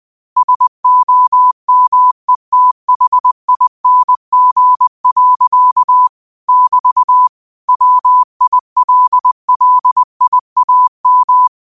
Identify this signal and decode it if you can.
Morse code, heard as a keyed tone, about 20 wpm, 1000 Hz. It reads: SOMETHING. <BT> WILLIAM
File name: qotdfiles / 20wpm